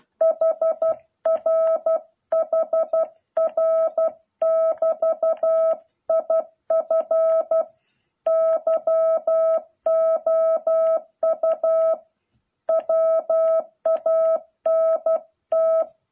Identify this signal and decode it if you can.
Morse code, heard as a keyed tone, about 12 wpm, 660 Hz. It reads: HR HR <BT> IF YOU WANT